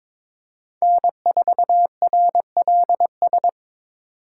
Morse code, heard as a keyed tone, about 25 wpm, 700 Hz. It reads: N4RLS
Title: Speed Racing - USA 1x3 Call Signs 22wpm